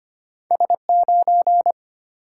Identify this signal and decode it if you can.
Morse code, heard as a keyed tone, about 25 wpm, 700 Hz. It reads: S9
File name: Two Letter-Number Combinations - 25wpm